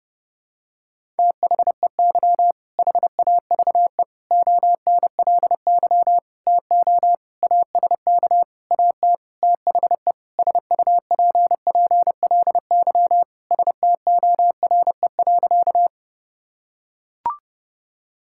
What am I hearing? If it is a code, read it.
Morse code, heard as a keyed tone, about 30 wpm, 700 Hz. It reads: THEY HAVE ONLY TO ASK AT THE SUPPLY STORE.
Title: Sentences from Top 1000 Words - Review Part 5 - 30wpm